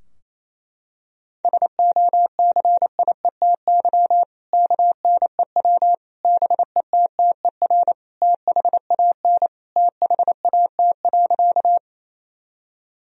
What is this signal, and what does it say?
Morse code, heard as a keyed tone, about 28 wpm, 700 Hz. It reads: SOCIETY KNEW BETTER THAN THAT.
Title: Sentences from Top 1000 Words - Review Part 4 - 28wpm